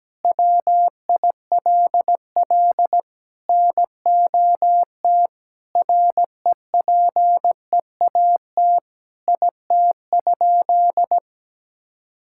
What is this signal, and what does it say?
Morse code, heard as a keyed tone, about 17 wpm, 700 Hz. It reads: WILL NOT REPEAT IT?
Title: Sentences from Top 1600 Words 17wpm